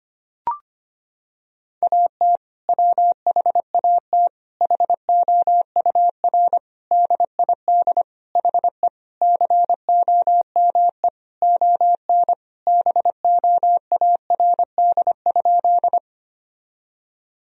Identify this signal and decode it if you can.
Morse code, heard as a keyed tone, about 25 wpm, 700 Hz. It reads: AT WHAT HOUR DID HE COME ON BOARD?
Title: Sentences from Top 1000 Words - Review Part 4 - 25wpm